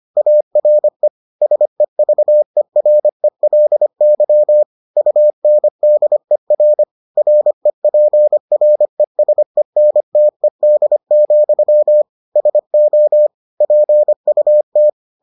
Morse code, heard as a keyed tone, about 25 wpm, 600 Hz. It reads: ARE SEVERELY UNDER REPRESENTED, SO PUT